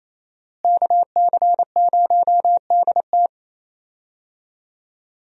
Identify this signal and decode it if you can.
Morse code, heard as a keyed tone, about 28 wpm, 700 Hz. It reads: KC0DT
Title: USA Call Signs - 5 Characters - Repeated 3x - 28wpm